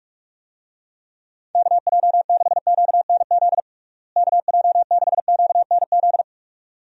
Morse code, heard as a keyed tone, about 46 wpm, 700 Hz. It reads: KJ6QNZ KJ6QNZ
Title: USA Call Signs - 6 Characters - Repeated 2x - 45wpm